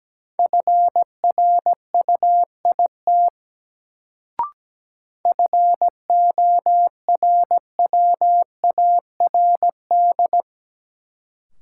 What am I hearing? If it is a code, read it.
Morse code, heard as a keyed tone, about 17 wpm, 700 Hz. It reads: FRUIT FORWARD
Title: Sentences from Top 1300 Words - New Word Drill - 17wpm